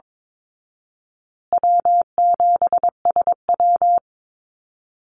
Morse code, heard as a keyed tone, about 22 wpm, 700 Hz. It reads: W7SW
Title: Call Signs Repeated 2x - 22wpm